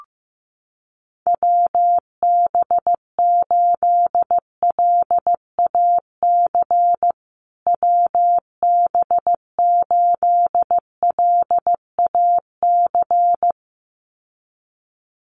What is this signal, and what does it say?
Morse code, heard as a keyed tone, about 15 wpm, 700 Hz. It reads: WB8LAC WB8LAC
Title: USA Call Signs Repeated 2x 15wpm